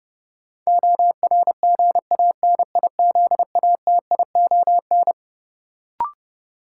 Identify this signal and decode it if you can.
Morse code, heard as a keyed tone, about 30 wpm, 700 Hz. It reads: ORGANIZATION
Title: Sentences from Top 1100 Words - New Word Drill Alt - 30wpm